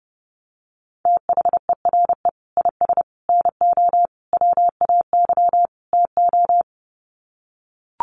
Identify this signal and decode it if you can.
Morse code, heard as a keyed tone, about 30 wpm, 700 Hz. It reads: THERE IS NO WAY TO
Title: Sets of 5 Words Encore 30wpm